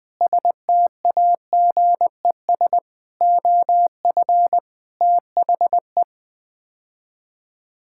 Morse code, heard as a keyed tone, about 20 wpm, 700 Hz. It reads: STAGES OF THE
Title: Sets of 6 Words - 20wpm